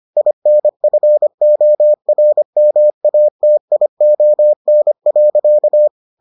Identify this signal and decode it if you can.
Morse code, heard as a keyed tone, about 25 wpm, 600 Hz. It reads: INFORMATION.